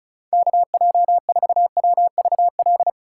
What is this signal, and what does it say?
Morse code, heard as a keyed tone, about 34 wpm, 700 Hz. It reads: KJ4WVL
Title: USA Call Signs - 6 Characters - 3x Character Spacing - 35wpm